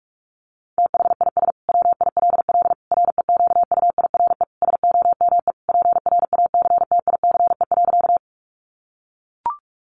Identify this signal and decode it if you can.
Morse code, heard as a keyed tone, about 44 wpm, 700 Hz. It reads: THIS WILL REQUIRE SOME PRACTICE.
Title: Sentences from Top 900 Words 45wpm